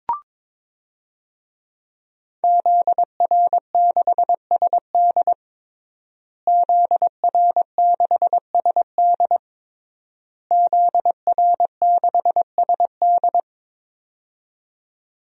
Morse code, heard as a keyed tone, about 22 wpm, 700 Hz. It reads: ZR6SD ZR6SD ZR6SD
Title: DXCC-100 Call Signs - 5 Characters - Repeated 3x - 22wpm